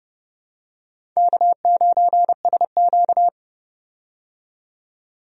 Morse code, heard as a keyed tone, about 30 wpm, 700 Hz. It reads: K9SQ
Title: USA Call Signs - 4 Characters - 30wpm